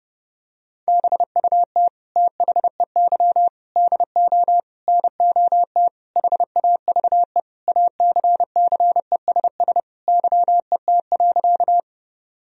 Morse code, heard as a keyed tone, about 31 wpm, 700 Hz. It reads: BUT THEY DO NOT HAVE ACCESS YET.
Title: Sentences from Top 1000 Words - Review Part 4 - 30wpm